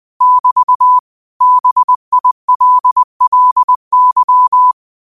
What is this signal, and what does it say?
Morse code, heard as a keyed tone, about 20 wpm, 1000 Hz. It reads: <BT> BILLY